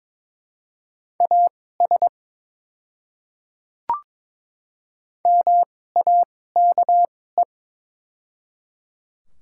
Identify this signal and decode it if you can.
Morse code, heard as a keyed tone, about 22 wpm, spaced as for 16 wpm, 700 Hz. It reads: AS MAKE
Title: 2x Character Spacing - Top 300 Words - 22wpm